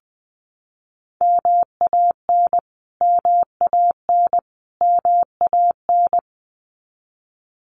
Morse code, heard as a keyed tone, about 20 wpm, 700 Hz. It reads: MAN MAN MAN
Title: Top 500 Words Repeated 20wpm